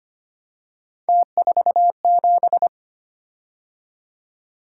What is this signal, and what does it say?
Morse code, heard as a keyed tone, about 25 wpm, 700 Hz. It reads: T47
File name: Three Letter-Number Combinations - 25wpm